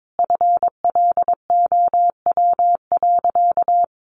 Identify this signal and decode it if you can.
Morse code, heard as a keyed tone, about 22 wpm, 700 Hz. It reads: FLOW.
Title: Sentences from Top 1200 Words 22wpm